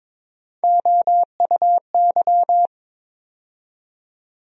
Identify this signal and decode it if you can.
Morse code, heard as a keyed tone, about 21 wpm, 700 Hz. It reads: OUY
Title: USA Call Sign Postfixes - 3 Letters 22wpm